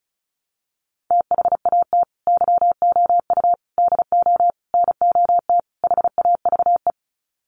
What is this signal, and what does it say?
Morse code, heard as a keyed tone, about 34 wpm, 700 Hz. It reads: THAT YOU DO NOT HAVE
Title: Sets of 5 Words Encore 35wpm